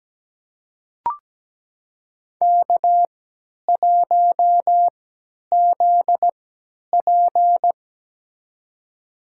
Morse code, heard as a keyed tone, about 17 wpm, spaced as for 10 wpm, 700 Hz. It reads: K1ZP
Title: USA Call Signs - 4 Characters - 3x Character Spacing - 17wpm